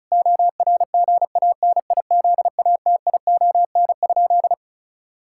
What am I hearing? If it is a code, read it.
Morse code, heard as a keyed tone, about 36 wpm, 700 Hz. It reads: ORGANIZATION?
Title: Sentences from Top 1000 Words - Review Part 5 - 35wpm